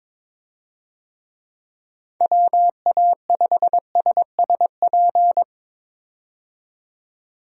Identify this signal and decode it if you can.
Morse code, heard as a keyed tone, about 22 wpm, 700 Hz. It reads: WA5SSP
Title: USA Call Signs - 6 Characters - 22wpm